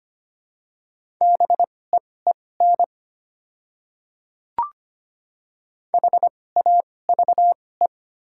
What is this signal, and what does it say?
Morse code, heard as a keyed tone, about 25 wpm, spaced as for 18 wpm, 700 Hz. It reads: BEEN HAVE
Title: 2x Character Spacing - Top 100 Words - 25wpm